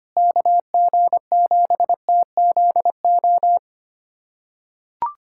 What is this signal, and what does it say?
Morse code, heard as a keyed tone, about 25 wpm, 700 Hz. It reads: KG7TZO
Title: USA Callsign DE Callsign - 6 Characters - 25wpm